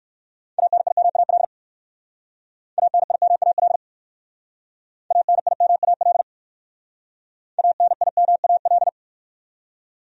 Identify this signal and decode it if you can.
Morse code, heard as a keyed tone, about 55 wpm, 700 Hz. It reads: ANIMAL ANIMAL ANIMAL ANIMAL
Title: Speed-Racing Top 200 Words 45wpm